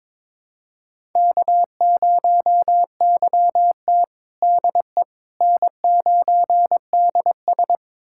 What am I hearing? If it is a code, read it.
Morse code, heard as a keyed tone, about 21 wpm, 700 Hz. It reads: K0YT DE N9DS
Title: USA Callsign DE Callsign - 4 Characters - 22wpm